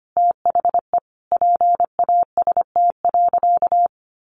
Morse code, heard as a keyed tone, about 25 wpm, 700 Hz. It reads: THE PAST.
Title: Sentences from Top 1200 Words 25wpm